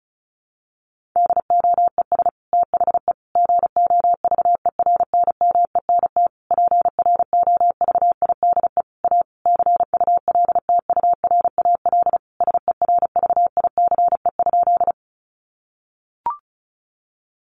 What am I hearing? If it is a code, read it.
Morse code, heard as a keyed tone, about 34 wpm, 700 Hz. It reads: DOES THE GOVERNMENT PROVIDE A CULTURAL SERVICE?
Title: Sentences from Top 1500 Words 35wpm